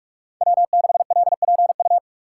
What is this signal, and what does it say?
Morse code, heard as a keyed tone, about 46 wpm, 700 Hz. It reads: W6LPU
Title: USA Call Signs - 5 Characters - 2x Character Spacing - 45wpm